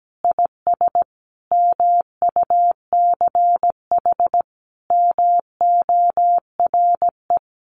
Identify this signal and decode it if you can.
Morse code, heard as a keyed tone, about 17 wpm, 700 Hz. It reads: IS MUCH MORE
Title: Sentences from Top 1200 Words 17wpm